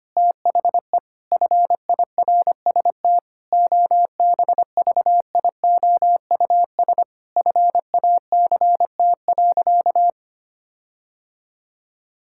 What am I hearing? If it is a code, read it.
Morse code, heard as a keyed tone, about 25 wpm, 700 Hz. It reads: THE FIRST OBVIOUS FACT.
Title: Sentences from Top 1600 Words 25wpm